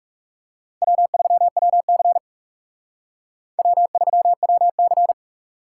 Morse code, heard as a keyed tone, about 36 wpm, 700 Hz. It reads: W3WC W3WC
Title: Speed Racing - USA 1x2 Call Signs 30wpm